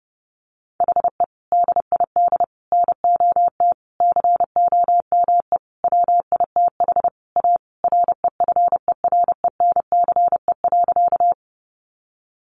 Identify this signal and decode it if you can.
Morse code, heard as a keyed tone, about 30 wpm, 700 Hz. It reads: HE DID NOT COME WITH A REFERENCE.
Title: Sentences from Top 1200 Words 30wpm